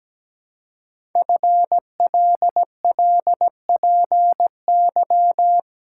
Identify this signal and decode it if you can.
Morse code, heard as a keyed tone, about 17 wpm, 700 Hz. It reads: FLLPY
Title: Code Groups - 5 Letters - 17wpm